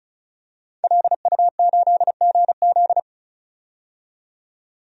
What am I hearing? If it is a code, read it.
Morse code, heard as a keyed tone, about 34 wpm, 700 Hz. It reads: LU8GZ
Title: DXCC-100 Call Signs - 5 Characters - 35wpm